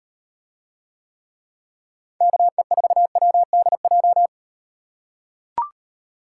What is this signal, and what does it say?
Morse code, heard as a keyed tone, about 38 wpm, 700 Hz. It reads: KE4WDJ